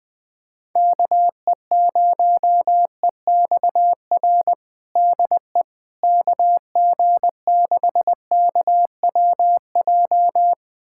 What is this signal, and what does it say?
Morse code, heard as a keyed tone, about 20 wpm, 700 Hz. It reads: KE0EXR DE KG6KWJ
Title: USA Callsign DE Callsign - 6 Characters - 20wpm